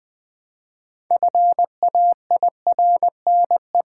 Morse code, heard as a keyed tone, about 20 wpm, 700 Hz. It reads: FAIRNE
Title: Build A Word - 2 Letters At A Time - 8-Letter Words - 20wpm